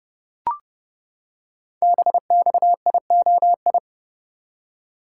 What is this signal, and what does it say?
Morse code, heard as a keyed tone, about 28 wpm, 700 Hz. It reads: BXIOI
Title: Code Groups - 5 Letters - 30wpm